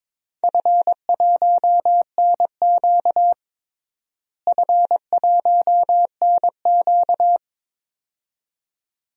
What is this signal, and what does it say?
Morse code, heard as a keyed tone, about 22 wpm, 700 Hz. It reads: F1NQ F1NQ
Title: DXCC-100 Call Signs - 4 Characters - Repeated 2x - 22wpm